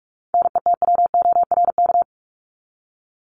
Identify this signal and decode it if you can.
Morse code, heard as a keyed tone, about 44 wpm, 700 Hz. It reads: NETWORK
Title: Top 751-1000 Words 45wpm